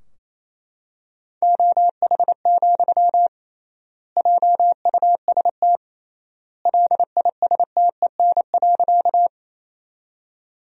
Morse code, heard as a keyed tone, about 27 wpm, 700 Hz. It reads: OH, JUST LISTEN.
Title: 3x Word-Spacing - Sentences from Top 500 Words - 28wpm